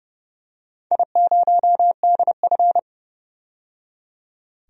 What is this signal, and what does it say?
Morse code, heard as a keyed tone, about 29 wpm, 700 Hz. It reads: I0DF
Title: DXCC-100 Call Signs - 4 Characters - 30wpm